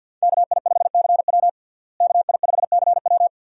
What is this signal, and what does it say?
Morse code, heard as a keyed tone, about 51 wpm, 700 Hz. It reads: KI5CW KI5CW
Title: USA Call Signs - 5 Characters - Repeated 2x - 50wpm